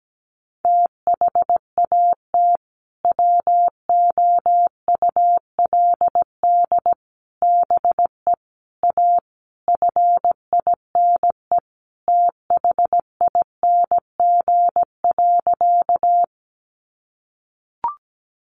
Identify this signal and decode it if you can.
Morse code, heard as a keyed tone, about 17 wpm, 700 Hz. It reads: THAT WOULD BE A FINE THING.
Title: Sentences from Top 500 Words 17wpm